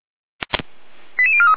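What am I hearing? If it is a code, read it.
Five-tone selective call (ZVEI), heard as quick runs of five tones, tones of about 80 ms. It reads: 88043
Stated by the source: ZVEI five-tone call received with an SDR.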